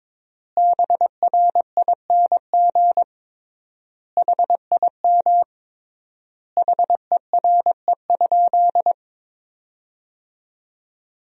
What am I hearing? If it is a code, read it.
Morse code, heard as a keyed tone, about 22 wpm, 700 Hz. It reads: BRING HIM HERE?
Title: 3x Word-Spacing - Sentences from Top 500 Words - 22wpm